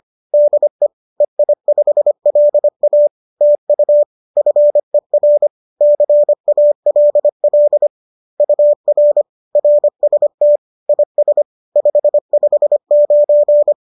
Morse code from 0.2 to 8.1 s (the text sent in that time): DE EI5LA TU FER CALL